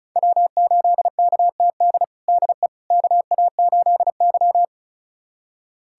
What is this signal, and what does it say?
Morse code, heard as a keyed tone, about 36 wpm, 700 Hz. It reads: W8KTD DE KA8Y
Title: USA Callsign DE Callsign 35wpm